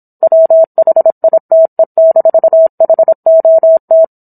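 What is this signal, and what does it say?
Morse code, heard as a keyed tone, about 27 wpm, 650 Hz. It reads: WHITE-HOT
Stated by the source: Morse code practice of a book chapter at 26 WPM.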